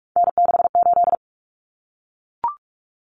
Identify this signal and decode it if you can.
Morse code, heard as a keyed tone, about 44 wpm, 700 Hz. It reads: N68
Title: Common Cut Numbers - 3-Digit 45wpm